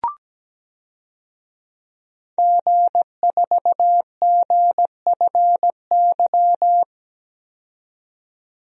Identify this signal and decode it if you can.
Morse code, heard as a keyed tone, about 17 wpm, 700 Hz. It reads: G4GFY
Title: DXCC-100 Call Signs - 5 Characters - 17wpm